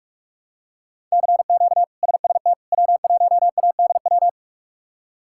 Morse code, heard as a keyed tone, about 43 wpm, 700 Hz. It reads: CQ SST W1ADW
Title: K1USN SST QSO Practice - 45wpm